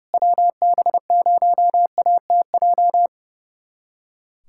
Morse code, heard as a keyed tone, about 30 wpm, 700 Hz. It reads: WB0ATJ
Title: USA Callsign DE Callsign 30wpm